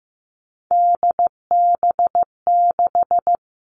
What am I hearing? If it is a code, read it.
Morse code, heard as a keyed tone, about 15 wpm, 700 Hz. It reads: DB6
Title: All Cut Numbers - 3-Digit 15wpm